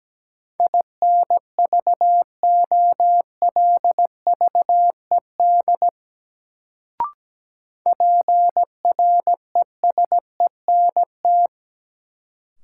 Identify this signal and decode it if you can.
Morse code, heard as a keyed tone, about 17 wpm, 700 Hz. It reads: INVOLVED PRESENT
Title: Sentences from Top 800 Words - New Word Drill - 17wpm